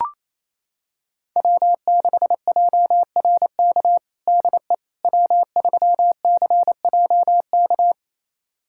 Morse code, heard as a keyed tone, about 28 wpm, 700 Hz. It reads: W6JRK DE W3CJK
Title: USA Callsign DE Callsign - 5 Characters - 28wpm